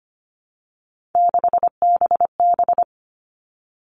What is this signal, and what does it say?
Morse code, heard as a keyed tone, about 25 wpm, 700 Hz. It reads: 6BB
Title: All Cut Numbers - 3-Digit 25wpm